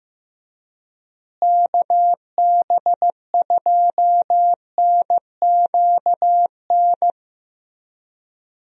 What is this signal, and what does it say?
Morse code, heard as a keyed tone, about 15 wpm, 700 Hz. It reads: KB2NQN
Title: USA Call Signs - 6 Characters - Repeated 2x - 15wpm